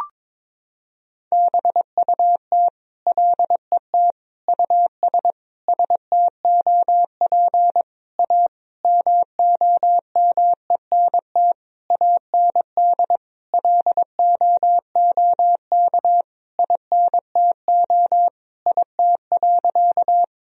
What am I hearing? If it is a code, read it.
Morse code, heard as a keyed tone, about 21 wpm, 700 Hz. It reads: BUT LET US STOP A MOMENT AND LOOK INTO IT.
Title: Sentences from Top 1000 Words - Review Part 1 - 22wpm